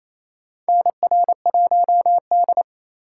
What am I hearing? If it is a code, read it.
Morse code, heard as a keyed tone, about 28 wpm, 700 Hz. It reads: NR1D
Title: USA Call Signs - 4 Characters - 4x Character Spacing - 28wpm